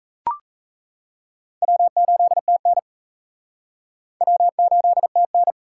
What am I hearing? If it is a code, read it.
Morse code, heard as a keyed tone, about 38 wpm, 700 Hz. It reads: W8TN W8TN